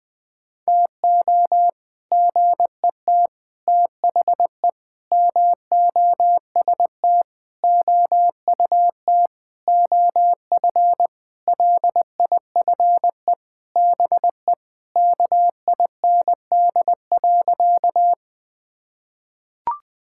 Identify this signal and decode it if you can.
Morse code, heard as a keyed tone, about 20 wpm, 700 Hz. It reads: TO GET THE MOST OUT OF LIFE BE KIND.
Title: Sentences with 6-Word Phrases - Spoken Beginning - 20wpm